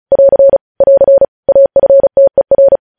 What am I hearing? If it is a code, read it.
Morse code, heard as a keyed tone, about 36 wpm, 550 Hz. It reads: <AR> <AR> AFTER
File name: podcast / News headlines-1x-35x35